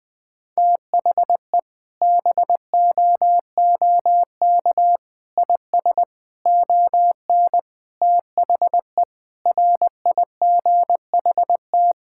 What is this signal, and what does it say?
Morse code, heard as a keyed tone, about 20 wpm, 700 Hz. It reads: THE BOOK IS ON THE RIGHT
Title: Sentences with 7-Word Phrases - 20wpm